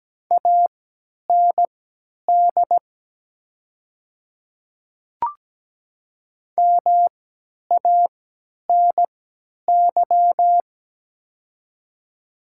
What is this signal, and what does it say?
Morse code, heard as a keyed tone, about 17 wpm, spaced as for 10 wpm, 700 Hz. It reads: AND MANY
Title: 3x Character Spacing - Top 100 Words - 17wpm